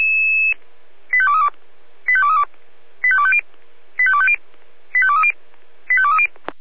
Five-tone selective call (ZVEI), heard as quick runs of five tones, tones of about 70 ms. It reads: 85232 85232 85250 85250 85230 85230
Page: ZVEI